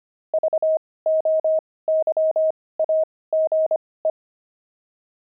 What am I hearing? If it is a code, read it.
Morse code, heard as a keyed tone, about 25 wpm, spaced as for 18 wpm, 630 Hz. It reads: VOYAGE
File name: pangram-fs-18-cs-25-blimp